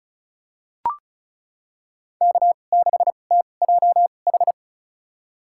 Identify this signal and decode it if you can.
Morse code, heard as a keyed tone, about 36 wpm, spaced as for 25 wpm, 700 Hz. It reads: K6TJH
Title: USA Call Signs - 5 Characters - 2x Character Spacing - 35wpm